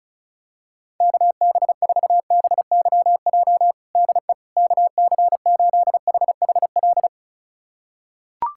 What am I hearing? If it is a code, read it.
Morse code, heard as a keyed tone, about 36 wpm, 700 Hz. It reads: KB4BYJ DE KC8HHL
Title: USA Callsign DE Callsign 35wpm